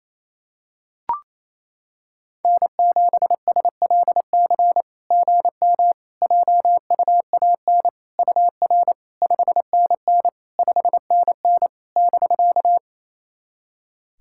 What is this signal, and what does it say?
Morse code, heard as a keyed tone, about 28 wpm, 690 Hz. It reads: N7SLC GM JUAN UR 5NN 5NN <BK>
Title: SOTA QSO Practice 28wpm